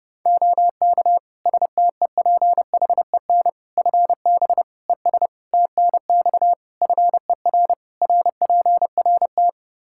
Morse code, heard as a keyed tone, about 30 wpm, 700 Hz. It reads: OK STEPHEN FB ES TNX FER RPRT
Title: Practice QSO - Exchange 2 - Info - 30wpm